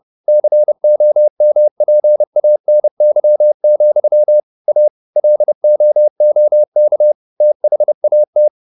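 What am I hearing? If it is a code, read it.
Morse code, heard as a keyed tone, about 30 wpm, 600 Hz. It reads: COMPANY, A LOOK THAT
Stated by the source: Morse code at 29 wpm, sending Ken Follett's novel Never in English, from CW med Gnister.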